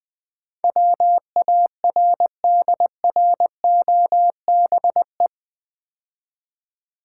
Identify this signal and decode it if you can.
Morse code, heard as a keyed tone, about 20 wpm, 700 Hz. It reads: WARDROBE
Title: Easy Words - Speed-Racing with Hints - 20wpm